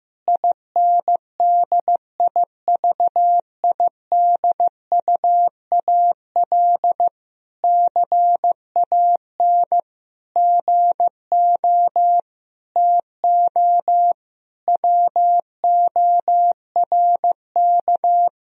Morse code, heard as a keyed tone, about 16 wpm, 700 Hz. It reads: INDIVIDUAL CAN GO TO WORK
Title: Sentences from Top 1000 Words - Review Part 4 - 15wpm